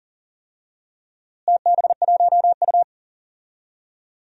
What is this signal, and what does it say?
Morse code, heard as a keyed tone, about 38 wpm, 700 Hz. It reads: TB1U
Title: DXCC-100 Call Signs - 4 Characters - Repeated 2x - 40wpm